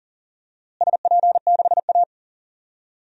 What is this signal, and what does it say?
Morse code, heard as a keyed tone, about 40 wpm, 700 Hz. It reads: SP6A
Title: DXCC-100 Call Signs - 4 Characters - Repeated 2x - 40wpm